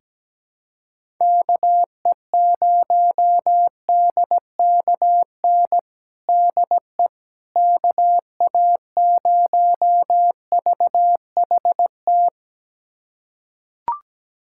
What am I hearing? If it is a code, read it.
Morse code, heard as a keyed tone, about 17 wpm, 700 Hz. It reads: KE0DKN DE KA0VHT
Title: USA Callsign DE Callsign - 6 Characters - 17wpm